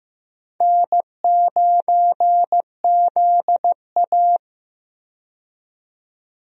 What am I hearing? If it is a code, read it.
Morse code, heard as a keyed tone, about 15 wpm, 700 Hz. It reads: N9ZA
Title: USA Call Signs - 4 Characters - 15wpm